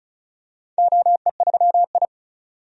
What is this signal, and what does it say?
Morse code, heard as a keyed tone, about 36 wpm, 700 Hz. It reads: OE3I
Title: DXCC-100 Call Signs - 4 Characters - 35wpm